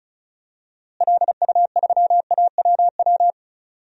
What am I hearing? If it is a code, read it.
Morse code, heard as a keyed tone, about 36 wpm, 700 Hz. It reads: LU3AWW
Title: DXCC-100 Call Signs - 6 Characters - Repeated 2x - 35wpm